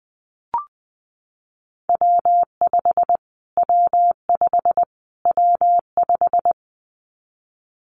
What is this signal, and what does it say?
Morse code, heard as a keyed tone, about 20 wpm, 700 Hz. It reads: W5 W5 W5
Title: Callsign Prefixes for Top 25 DXCC - Repeated 3x - 20wpm